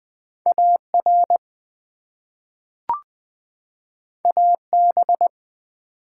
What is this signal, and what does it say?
Morse code, heard as a keyed tone, about 20 wpm, 700 Hz. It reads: AR AB
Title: ARRL Sections v2 - 20wpm